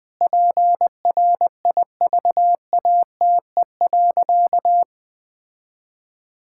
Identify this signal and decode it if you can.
Morse code, heard as a keyed tone, about 20 wpm, 700 Hz. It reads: PRIVATE.
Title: Sentences from Top 1600 Words 20wpm